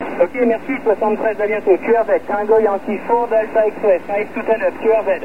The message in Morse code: none